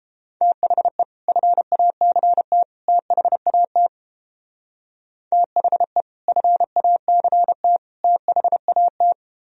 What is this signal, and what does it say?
Morse code, heard as a keyed tone, about 31 wpm, 700 Hz. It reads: THE FACT THAT THE FACT THAT